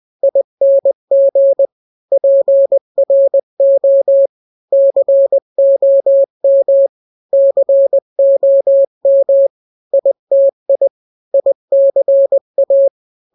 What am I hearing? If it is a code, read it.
Morse code, heard as a keyed tone, about 20 wpm, 550 Hz. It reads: ING PRO COM COM ITI ICA
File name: n-grams / 3-grams 20 18 WPM